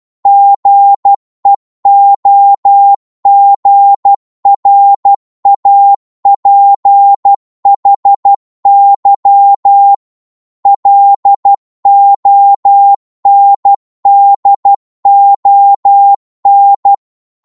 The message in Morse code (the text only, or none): GEOGRAPHY LONDON